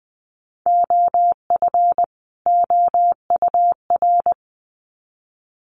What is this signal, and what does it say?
Morse code, heard as a keyed tone, about 20 wpm, 700 Hz. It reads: OF OUR
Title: Sets of 2 Words 20wpm